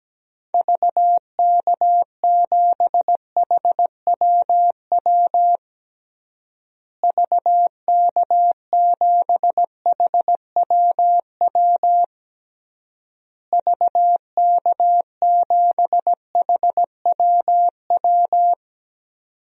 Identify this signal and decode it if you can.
Morse code, heard as a keyed tone, about 17 wpm, 700 Hz. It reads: VK7HWW VK7HWW VK7HWW
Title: DXCC-100 Call Signs - 6 Characters - Repeated 3x - 17wpm